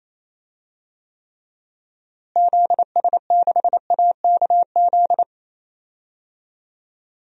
Morse code, heard as a keyed tone, about 28 wpm, 700 Hz. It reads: ZS6AKZ
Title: DXCC-100 Call Signs - 6 Characters - Repeated 3x - 28wpm